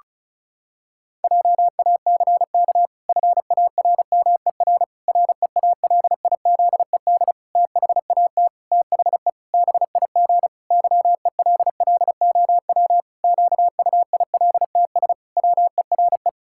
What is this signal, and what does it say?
Morse code, heard as a keyed tone, about 36 wpm, 700 Hz. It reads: JACK FARMER REALIZED THAT THE BIG YELLOW QUILTS WERE
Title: Pangram Challenge v2 - 35wpm